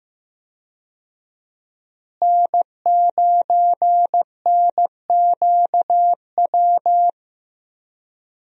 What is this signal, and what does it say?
Morse code, heard as a keyed tone, about 15 wpm, 700 Hz. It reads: N9NQW
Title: USA Call Signs - 5 Characters - Repeated 3x - 15wpm